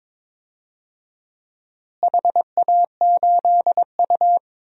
Morse code, heard as a keyed tone, about 21 wpm, 700 Hz. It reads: HA8U
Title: DXCC-100 Call Signs - 4 Characters - Repeated 2x - 22wpm